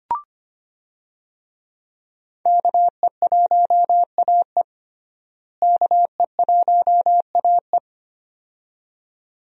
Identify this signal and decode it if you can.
Morse code, heard as a keyed tone, about 25 wpm, 700 Hz. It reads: KE1AE KE1AE
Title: USA Call Signs - 5 Characters - Repeated 2x - 25wpm